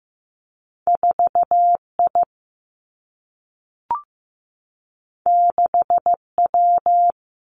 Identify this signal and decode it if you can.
Morse code, heard as a keyed tone, about 15 wpm, 700 Hz. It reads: 4 I 6 W
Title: Two Challenging Letter-Number Combinations 15wpm